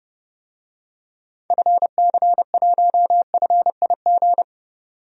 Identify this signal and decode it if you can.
Morse code, heard as a keyed tone, about 31 wpm, 700 Hz. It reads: FC1FIG
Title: DXCC-100 Call Signs - 6 Characters - 30wpm